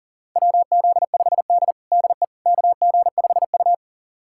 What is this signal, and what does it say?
Morse code, heard as a keyed tone, about 40 wpm, 700 Hz. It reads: WZ5D DE KG5U
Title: USA Callsign DE Callsign - 4 Characters - 40wpm